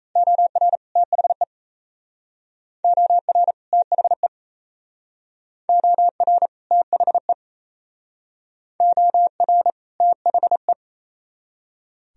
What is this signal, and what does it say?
Morse code, heard as a keyed tone, about 36 wpm, 700 Hz. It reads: OR THE OR THE OR THE OR THE